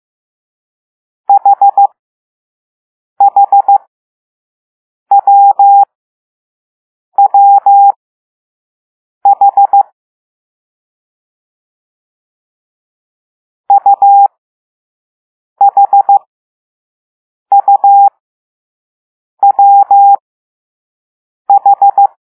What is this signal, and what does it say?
Morse code, heard as a keyed tone, about 15 wpm, spaced as for 5 wpm, 800 Hz. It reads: HHWWH UHUWH